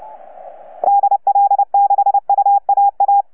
Morse code, heard as a keyed tone, about 29 wpm, 810 Hz. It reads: DL6UAA
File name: UP2L-20-CW